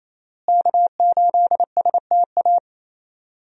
Morse code, heard as a keyed tone, about 27 wpm, 700 Hz. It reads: K8STA
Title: USA Call Signs - 5 Characters - Repeated 2x - 28wpm